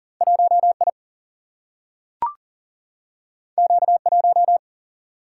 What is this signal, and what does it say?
Morse code, heard as a keyed tone, about 40 wpm, 700 Hz. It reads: 1 I Q 1
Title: Single-Letter Single-Number 40wpm